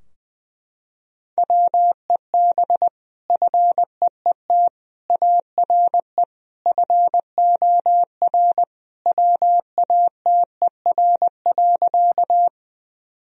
Morse code, heard as a keyed tone, about 20 wpm, 700 Hz. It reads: WEB FEET ARE FOR WATER.
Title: Sentences from Top 1000 Words - Review Part 4 - 20wpm